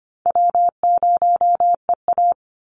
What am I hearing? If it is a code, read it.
Morse code, heard as a keyed tone, about 25 wpm, 700 Hz. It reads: W0EA
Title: Easy Call Signs Repeated 2x - 25wpm